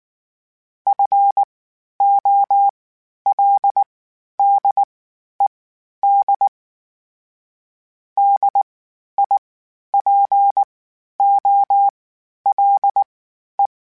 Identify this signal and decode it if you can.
Morse code, heard as a keyed tone, about 19 wpm, spaced as for 11 wpm, 800 Hz. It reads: FOLDED DIPOLE
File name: (spaced) / FromSMs spaced mp3s / Antennas sp@18